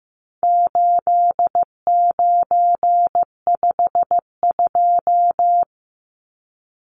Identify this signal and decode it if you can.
Morse code, heard as a keyed tone, about 15 wpm, 700 Hz. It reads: 8952
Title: Numbers - 4-Digit 15wpm